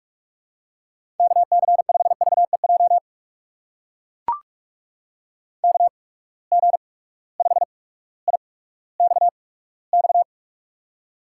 Morse code, heard as a keyed tone, about 44 wpm, spaced as for 19 wpm, 700 Hz. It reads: KC5VEJ KG5IXX
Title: USA Call Signs - 6 Characters - 8x Character Spacing - 45wpm